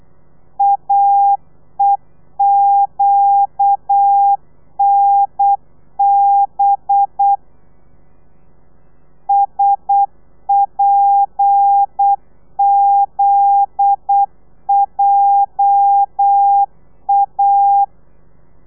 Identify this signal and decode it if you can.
Morse code, heard as a keyed tone, about 8 wpm, 800 Hz. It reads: AEQNB SPZJA
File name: telegrafia / i2tfj / Lez9 35